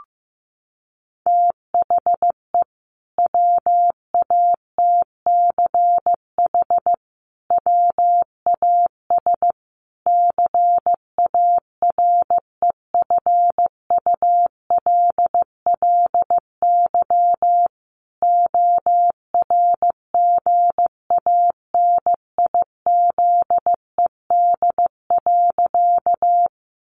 Morse code, heard as a keyed tone, about 15 wpm, 700 Hz. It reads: THE WATCH WAS CAREFULLY ORGANIZED.
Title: Sentences from Top 1300 Words 15wpm